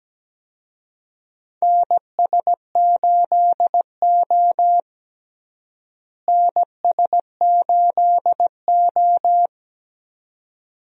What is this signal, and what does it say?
Morse code, heard as a keyed tone, about 17 wpm, 700 Hz. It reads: NS8O NS8O
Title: USA Call Signs - 4 Characters - Repeated 2x - 17wpm